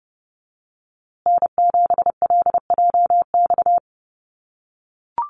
Morse code, heard as a keyed tone, about 30 wpm, 700 Hz. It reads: N7LJX
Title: USA Call Signs Repeated 2x 30wpm